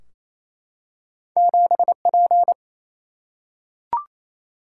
Morse code, heard as a keyed tone, about 28 wpm, 700 Hz. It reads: 7 P
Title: Single-Letter Single-Number 28wpm